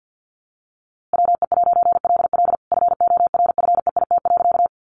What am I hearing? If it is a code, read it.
Morse code, heard as a keyed tone, about 50 wpm, 700 Hz. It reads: WE'LL FORFEIT.
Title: A Sound Of Thunder 50wpm